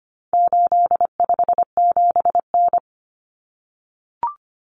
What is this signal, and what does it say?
Morse code, heard as a keyed tone, about 25 wpm, 700 Hz. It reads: 857N
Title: Common Cut Numbers - 4-Digit 25wpm